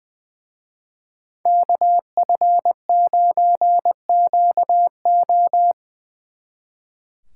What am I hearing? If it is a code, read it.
Morse code, heard as a keyed tone, about 22 wpm, 700 Hz. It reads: KF9QO
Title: Speed Racing - USA 2x2 Call Signs 20wpm